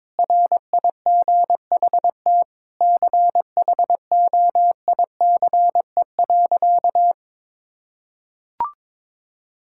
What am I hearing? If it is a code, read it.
Morse code, heard as a keyed tone, about 22 wpm, 700 Hz. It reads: RIGHT CHOICE.
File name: practice / 22wpm / Sentences with 8-Word Phrases - 22wpm